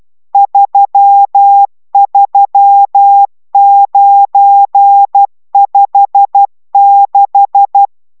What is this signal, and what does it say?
Morse code, heard as a keyed tone, about 12 wpm, 800 Hz. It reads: 33956